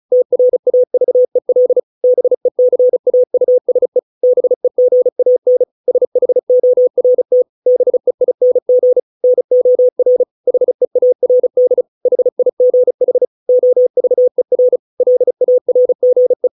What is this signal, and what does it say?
Morse code, heard as a keyed tone, about 34 wpm, 500 Hz. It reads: TRAVEL BECAUSE BEGAN SHORT BEING NOR HEARD HIGH OVER LARGE